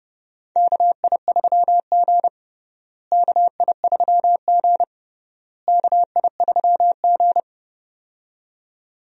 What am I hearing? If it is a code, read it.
Morse code, heard as a keyed tone, about 30 wpm, 700 Hz. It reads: KI3G KI3G KI3G
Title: USA Call Signs - 4 Characters - Repeated 3x - 30wpm